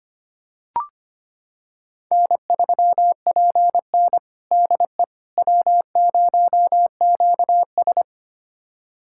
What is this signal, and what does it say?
Morse code, heard as a keyed tone, about 25 wpm, 700 Hz. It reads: N3PN DE W0QS
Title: USA Callsign DE Callsign - 4 Characters - 25wpm